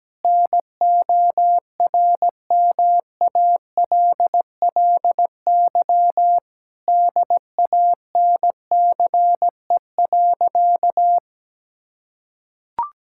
Morse code, heard as a keyed tone, about 17 wpm, 700 Hz. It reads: NORMALLY DANCE.
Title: Sentences from Top 1000 Words - Review Part 6 - 17wpm